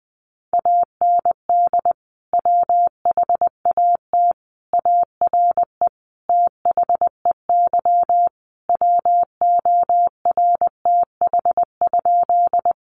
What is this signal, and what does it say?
Morse code, heard as a keyed tone, about 20 wpm, 700 Hz. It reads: AND WHAT ARE THEY WORTH?
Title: Sentences from Top 800 Words 20wpm